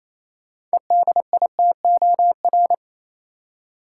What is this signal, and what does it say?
Morse code, heard as a keyed tone, about 28 wpm, 700 Hz. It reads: EDITOR
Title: Six Letter Words - 28wpm